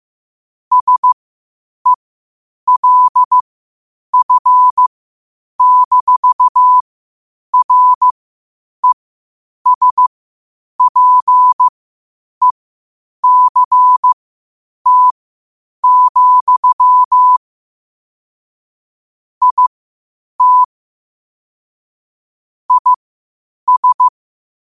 Morse code, heard as a keyed tone, about 15 wpm, spaced as for 8 wpm, 1000 Hz. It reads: SELF-RESPECT, IT IS
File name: qotdfiles / 8wpm